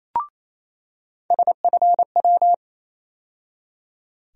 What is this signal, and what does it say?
Morse code, heard as a keyed tone, about 27 wpm, 700 Hz. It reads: SFW
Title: USA Call Sign Postfixes - 3 Letters 28wpm